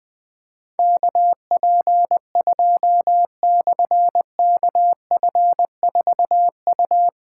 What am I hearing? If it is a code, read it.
Morse code, heard as a keyed tone, about 20 wpm, 700 Hz. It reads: KP2/KF4U
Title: Difficult Call Signs 20wpm